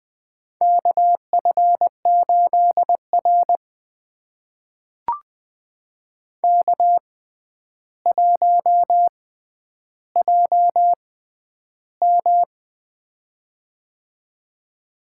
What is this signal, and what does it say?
Morse code, heard as a keyed tone, about 20 wpm, spaced as for 10 wpm, 700 Hz. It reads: KF8R K1JM
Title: USA Call Signs - 6x Character Spacing - 20wpm